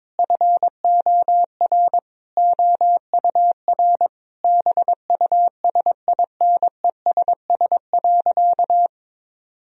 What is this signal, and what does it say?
Morse code, heard as a keyed tone, about 22 wpm, 700 Hz. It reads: FOR OUR BUSINESS.
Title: Sentences from Top 1000 Words - Review Part 4 - 22wpm